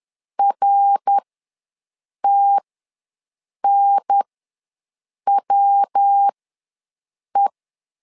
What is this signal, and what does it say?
Morse code, heard as a keyed tone, about 10 wpm, spaced as for 6 wpm, 800 Hz. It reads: RTNWE
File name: LEZIONI / 3 MWRO / 3 MWR ws 10wpm 2151